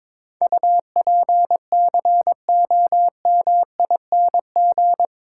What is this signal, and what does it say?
Morse code, heard as a keyed tone, about 22 wpm, 700 Hz. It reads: UPCOMING
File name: practice / 22wpm / Eight Letter Words - 22wpm